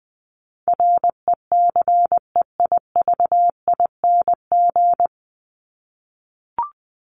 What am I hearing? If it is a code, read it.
Morse code, heard as a keyed tone, about 19 wpm, 700 Hz. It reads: RECEIVING
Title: ING Suffix - Top 100 Words with Base Words 20wpm